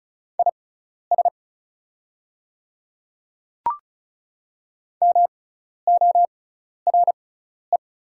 Morse code, heard as a keyed tone, about 34 wpm, spaced as for 12 wpm, 700 Hz. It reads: IS MORE
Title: 6x Character Spacing - Top 100 Words - 35wpm